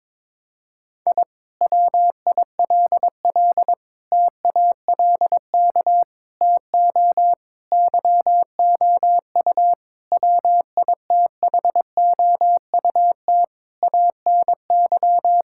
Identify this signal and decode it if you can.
Morse code, heard as a keyed tone, about 22 wpm, 700 Hz. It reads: I WILL TALK TO YOU WITHOUT ANY
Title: Sentences from Top 1000 Words - Review Part 6 - 22wpm